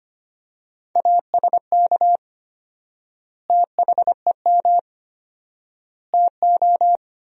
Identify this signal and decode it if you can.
Morse code, heard as a keyed tone, about 25 wpm, 700 Hz. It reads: ASK THEM TO
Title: 4x Word-Spacing - Sentences from Top 400 Words - 25wpm